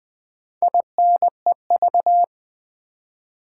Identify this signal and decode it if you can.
Morse code, heard as a keyed tone, about 20 wpm, 700 Hz. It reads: INEV
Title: Build A Word - 4 Letters At A Time - 10-Letter Words - 20wpm